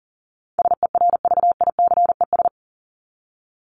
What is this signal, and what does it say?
Morse code, heard as a keyed tone, about 40 wpm, 700 Hz. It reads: SERVICES
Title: ES:S Suffix - Top 100 Words with Base Words 40wpm